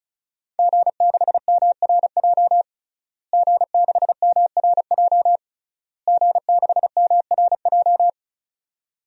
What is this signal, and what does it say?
Morse code, heard as a keyed tone, about 34 wpm, 700 Hz. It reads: G6MRJ G6MRJ G6MRJ
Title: DXCC-100 Call Signs - 5 Characters - Repeated 3x - 35wpm